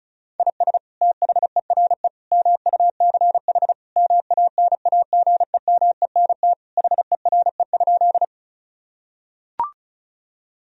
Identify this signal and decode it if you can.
Morse code, heard as a keyed tone, about 36 wpm, 700 Hz. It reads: IS THERE MUCH MANAGEMENT HERE?
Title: Sentences from Top 1000 Words - Review Part 4 - 35wpm